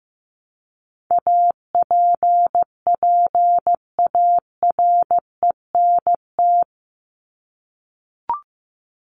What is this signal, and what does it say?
Morse code, heard as a keyed tone, about 15 wpm, 700 Hz. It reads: APPARENT
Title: LY Suffix - Top 100 Words with Base Words 15wpm